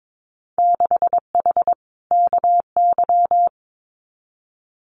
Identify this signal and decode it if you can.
Morse code, heard as a keyed tone, about 22 wpm, 700 Hz. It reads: 6H KY
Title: Winter Field Day Exchanges Encore 22wpm